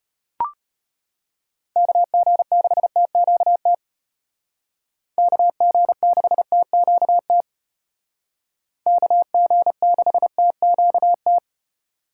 Morse code, heard as a keyed tone, about 33 wpm, 700 Hz. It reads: KG6TQT KG6TQT KG6TQT